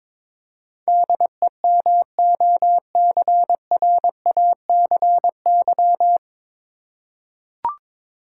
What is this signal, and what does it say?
Morse code, heard as a keyed tone, about 23 wpm, 700 Hz. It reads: DEMOCRACY
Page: Morse Code Ninja practice 22wpm